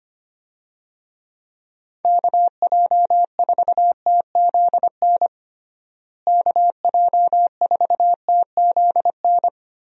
Morse code, heard as a keyed tone, about 25 wpm, 700 Hz. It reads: KJ4TZN KJ4TZN
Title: USA Call Signs - 6 Characters - Repeated 2x - 25wpm